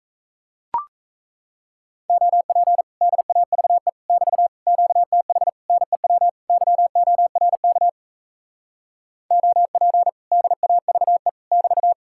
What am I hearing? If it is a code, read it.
Morse code, heard as a keyed tone, about 38 wpm, 700 Hz. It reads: OP DAVE <BT> QTH NEW YORK OP DAVE <BT>